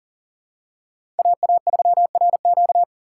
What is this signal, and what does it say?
Morse code, heard as a keyed tone, about 40 wpm, 700 Hz. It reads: AA3RQ
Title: USA Call Signs - 6x Character Spacing - 40wpm